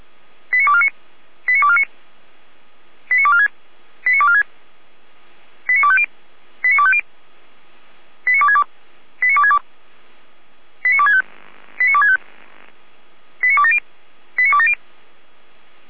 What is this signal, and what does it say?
Five-tone selective call (ZVEI), heard as quick runs of five tones, tones of about 70 ms. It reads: 89249 89249 89246 89246 89250 89250 89262 89262 89265 89265 89270 89270